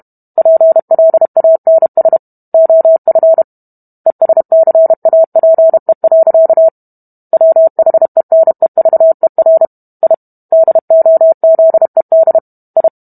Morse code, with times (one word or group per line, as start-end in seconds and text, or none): 0.4-2.2 PLANS
2.5-3.4 OF
4.1-6.7 ESCAPE.
7.3-9.7 WHENEVER
10.0-10.1 I
10.5-12.4 DOZED
12.8-12.9 I